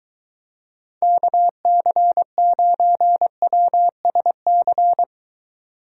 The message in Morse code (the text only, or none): KC9WSC